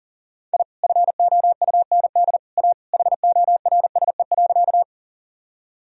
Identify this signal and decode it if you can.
Morse code, heard as a keyed tone, about 40 wpm, 700 Hz. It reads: I FOUND A HORSE.
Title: Sentences from Top 1000 Words - Review Part 1 - 40wpm